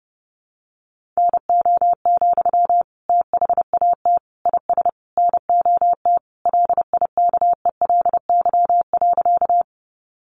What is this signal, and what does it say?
Morse code, heard as a keyed tone, about 30 wpm, 700 Hz. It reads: NO, THAT IS NOT LIKELY.
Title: Sentences from Top 600 Words 30wpm